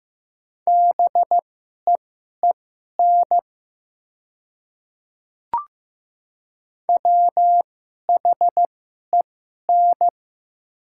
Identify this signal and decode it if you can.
Morse code, heard as a keyed tone, about 15 wpm, spaced as for 11 wpm, 700 Hz. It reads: BEEN WHEN
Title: 2x Character Spacing - Top 100 Words - 15wpm